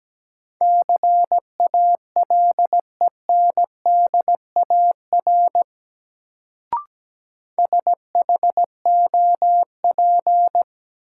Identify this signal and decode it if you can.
Morse code, heard as a keyed tone, about 17 wpm, 700 Hz. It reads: CALENDAR SHOP
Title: Sentences from Top 1600 Words - New Word Drill Alt - 17wpm